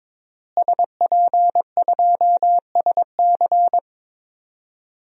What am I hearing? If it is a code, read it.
Morse code, heard as a keyed tone, about 22 wpm, 700 Hz. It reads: SP2SC
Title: DXCC-100 Call Signs - 5 Characters - Repeated 3x - 22wpm